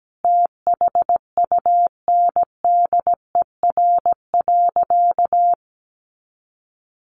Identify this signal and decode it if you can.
Morse code, heard as a keyed tone, about 17 wpm, 700 Hz. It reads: THUNDER.
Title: A Sound Of Thunder 17wpm